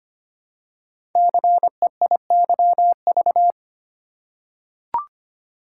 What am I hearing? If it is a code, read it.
Morse code, heard as a keyed tone, about 25 wpm, 700 Hz. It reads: CEIYV
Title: Code Groups - 5 Letter:Numbers - 25wpm